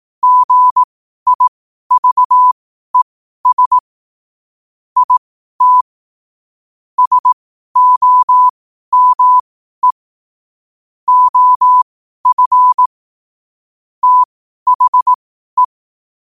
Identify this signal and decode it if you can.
Morse code, heard as a keyed tone, about 18 wpm, spaced as for 12 wpm, 1000 Hz. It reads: GIVES IT SOME OF THE
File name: qotdfiles / 12wpm